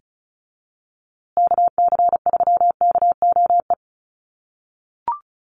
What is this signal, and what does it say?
Morse code, heard as a keyed tone, about 34 wpm, 700 Hz. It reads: KC3KOE
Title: USA Call Signs Repeated 3x 35wpm